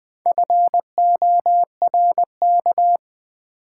Morse code, heard as a keyed tone, about 20 wpm, 700 Hz. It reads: FORK
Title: Build A Word - 2 Letters At A Time - 4-Letter Words - 20wpm